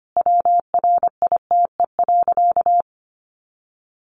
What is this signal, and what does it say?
Morse code, heard as a keyed tone, about 25 wpm, 700 Hz. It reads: WRITE.
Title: Sentences from Top 800 Words 25wpm